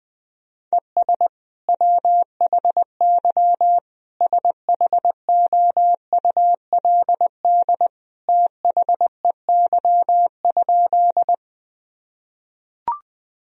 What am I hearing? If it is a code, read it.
Morse code, heard as a keyed tone, about 20 wpm, 700 Hz. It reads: ES WHY SHOULD THEY?
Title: Sentences from Top 300 Words and Abbreviations 20wpm